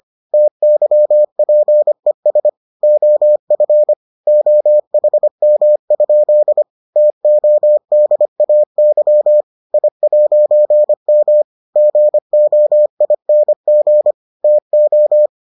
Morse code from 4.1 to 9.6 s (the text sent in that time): OHM? TODAY